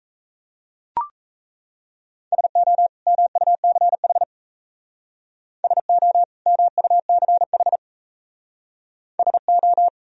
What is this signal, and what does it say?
Morse code, heard as a keyed tone, about 34 wpm, 700 Hz. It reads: SO MUCH SO MUCH SO
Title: Speed-Racing - Sets of 2 Words 28wpm